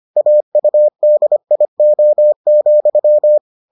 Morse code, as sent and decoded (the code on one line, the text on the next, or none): .- ..- -.. .. --- --..--
AUDIO,